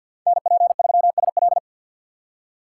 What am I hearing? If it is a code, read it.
Morse code, heard as a keyed tone, about 51 wpm, 700 Hz. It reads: NP3SL
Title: USA Call Signs - 5 Characters - 4x Character Spacing - 50wpm